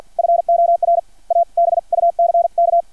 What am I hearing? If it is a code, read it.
Morse code, heard as a keyed tone, about 50 wpm, spaced as for 44 wpm, 680 Hz. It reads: FOR A BUCK
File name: archived / CRSnet